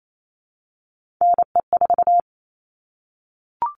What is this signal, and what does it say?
Morse code, heard as a keyed tone, about 28 wpm, 700 Hz. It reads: NE4
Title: All Cut Numbers - 3-Digit 28wpm